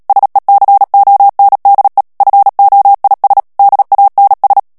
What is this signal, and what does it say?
Morse code, heard as a keyed tone, about 37 wpm, 800 Hz. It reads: SECONDE FOIS DANS
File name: CW / FICHIERS-MP3 / FRANCAIS TEXTE CLAIR 2 VITESSE 35 MOTS MINUTES